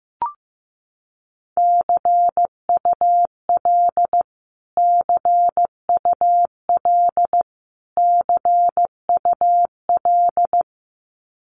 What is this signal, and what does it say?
Morse code, heard as a keyed tone, about 15 wpm, 700 Hz. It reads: CUL CUL CUL
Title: Abbreviations Repeated 3x - 15wpm